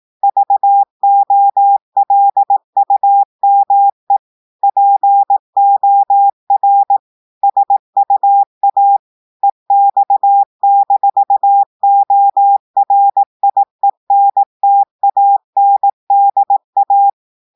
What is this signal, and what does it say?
Morse code, heard as a keyed tone, about 18 wpm, 800 Hz. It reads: VOLUME POR SUA EX-ORIENTANDA
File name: audios / portugais / portugais texte en clair 1 vitesse 18 mots minutes